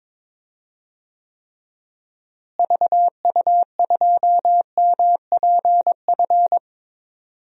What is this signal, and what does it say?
Morse code, heard as a keyed tone, about 22 wpm, 700 Hz. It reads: VU2MPF
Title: DXCC-100 Call Signs - 6 Characters - Repeated 2x - 22wpm